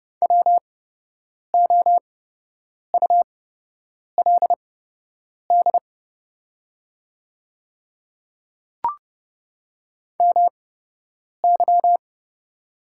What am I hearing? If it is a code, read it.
Morse code, heard as a keyed tone, about 30 wpm, spaced as for 8 wpm, 700 Hz. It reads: WOULD MY
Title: 8x Character Spacing - Top 200 Words - 30wpm